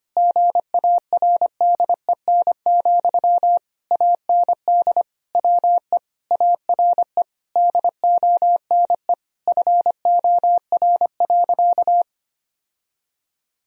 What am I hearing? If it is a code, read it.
Morse code, heard as a keyed tone, about 25 wpm, 690 Hz. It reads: GARDEN, AND WE ARE DONE FOR.
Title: Sentences from Top 1000 Words - Review Part 4 - 25wpm